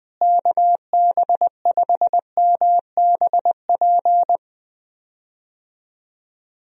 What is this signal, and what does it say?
Morse code, heard as a keyed tone, about 20 wpm, 700 Hz. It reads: KB5MBP
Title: USA Call Signs - 6 Characters - Repeated 2x - 20wpm